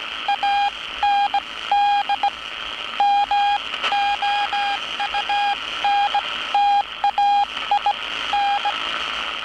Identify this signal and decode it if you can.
Morse code, heard as a keyed tone, about 18 wpm, spaced as for 15 wpm, 810 Hz. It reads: AND MOUNTAIN
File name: training / NAVPERS 11049to11060 / NAVPERS 11059 RA RB